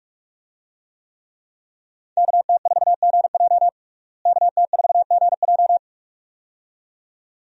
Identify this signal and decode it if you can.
Morse code, heard as a keyed tone, about 41 wpm, 700 Hz. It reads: KT4GJ KT4GJ
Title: USA Call Signs - 5 Characters - Repeated 2x - 45wpm